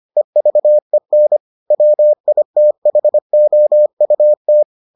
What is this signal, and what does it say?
Morse code, heard as a keyed tone, about 25 wpm, 600 Hz. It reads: EVEN WITHOUT